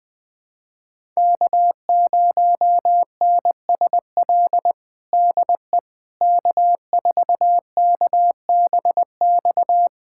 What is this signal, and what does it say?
Morse code, heard as a keyed tone, about 20 wpm, 700 Hz. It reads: K0NSL DE K4KBX
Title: USA Callsign DE Callsign - 5 Characters - 20wpm